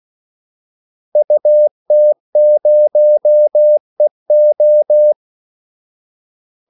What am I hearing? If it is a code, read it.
Morse code, heard as a keyed tone, about 16 wpm, 600 Hz. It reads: UT0EO
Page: Kaldesignaler 16 wpm